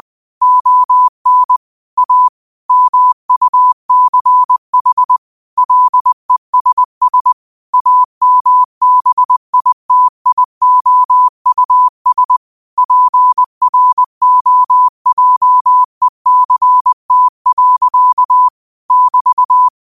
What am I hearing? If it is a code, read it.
Morse code, heard as a keyed tone, about 20 wpm, 1000 Hz. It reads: ON A MUCH LESS AMBITIOUS PROJECT. <BT>
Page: Quotes for Thu, 14 Aug 2025 in Morse Code at 20 words per minute.